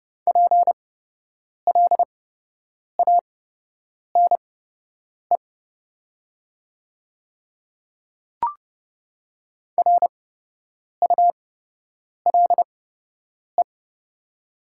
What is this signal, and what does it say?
Morse code, heard as a keyed tone, about 29 wpm, spaced as for 8 wpm, 700 Hz. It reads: PLANE RULE
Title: 8x Character Spacing - Top 500 Words - 30wpm